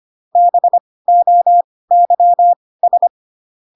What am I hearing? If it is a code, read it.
Morse code, heard as a keyed tone, about 25 wpm, spaced as for 18 wpm, 700 Hz. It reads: BOYS